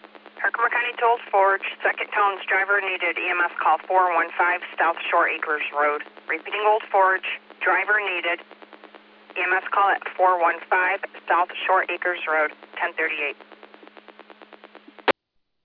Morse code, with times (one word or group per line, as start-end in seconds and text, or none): none